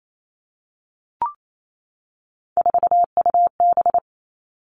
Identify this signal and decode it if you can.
Morse code, heard as a keyed tone, about 28 wpm, 700 Hz. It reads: 4UB
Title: All Cut Numbers - 3-Digit 28wpm